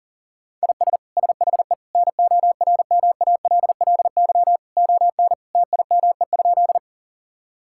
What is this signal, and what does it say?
Morse code, heard as a keyed tone, about 40 wpm, 700 Hz. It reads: IS SHE NORMALLY ON TIME?
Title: Sentences from Top 1000 Words - Review Part 6 - 40wpm